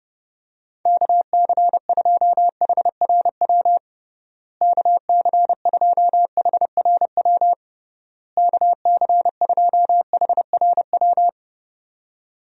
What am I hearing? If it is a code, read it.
Morse code, heard as a keyed tone, about 32 wpm, 700 Hz. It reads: KC2HRW KC2HRW KC2HRW
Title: USA Call Signs - 6 Characters - Repeated 3x - 30wpm